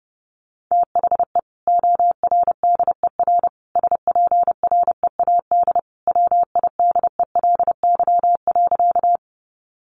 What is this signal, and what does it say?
Morse code, heard as a keyed tone, about 30 wpm, 700 Hz. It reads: THE ORDER SPREAD WIDELY.
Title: Sentences from Top 1200 Words 30wpm